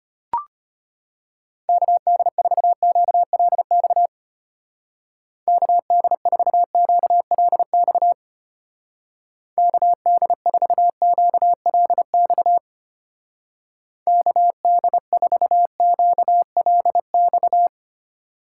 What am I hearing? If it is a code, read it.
Morse code, heard as a keyed tone, about 32 wpm, 700 Hz. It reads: KD4QLX KD4QLX KD4QLX KD4QLX